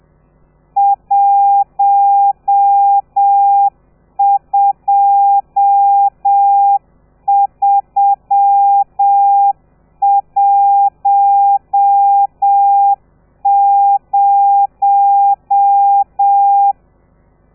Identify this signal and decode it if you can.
Morse code, heard as a keyed tone, about 7 wpm, 800 Hz. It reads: 12310
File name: telegrafia / i2tfj / Lez13 30